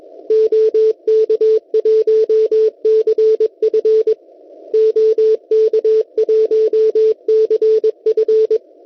Morse code, heard as a keyed tone, about 21 wpm, 440 Hz. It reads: OK1CF OK1CF
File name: Test Pixid 160m / kiwi.oe9.at 1825.90 cw